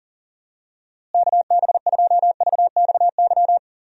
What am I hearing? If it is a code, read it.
Morse code, heard as a keyed tone, about 40 wpm, 700 Hz. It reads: KB2VXY
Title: USA Call Signs - 6 Characters - Repeated 3x - 40wpm